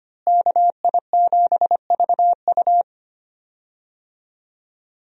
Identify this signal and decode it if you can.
Morse code, heard as a keyed tone, about 25 wpm, 690 Hz. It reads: KI7VU
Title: USA Call Signs - 5 Characters - Repeated 2x - 25wpm